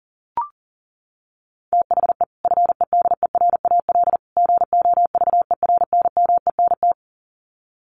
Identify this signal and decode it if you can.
Morse code, heard as a keyed tone, about 40 wpm, 700 Hz. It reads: THE FEDERAL GOVERNMENT
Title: Sets of 3 Words 40wpm